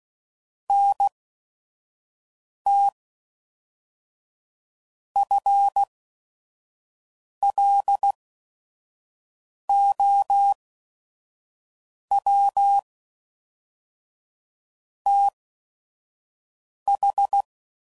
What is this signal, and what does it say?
Morse code, heard as a keyed tone, about 16 wpm, spaced as for 5 wpm, 800 Hz. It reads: NT FLOW TH
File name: CW Code Groups @5WPM / Technical Subjects / Reactance